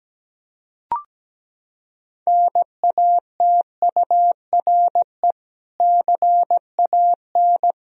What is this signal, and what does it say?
Morse code, heard as a keyed tone, about 17 wpm, 700 Hz. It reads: NATURE CAN
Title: Sentences from Top 1000 Words - Review Part 3 - 17wpm